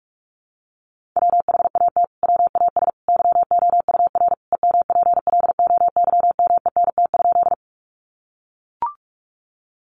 Morse code, heard as a keyed tone, about 44 wpm, 700 Hz. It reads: WHAT WAS YOUR EMPLOYMENT?
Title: Sentences from Top 1500 Words 45wpm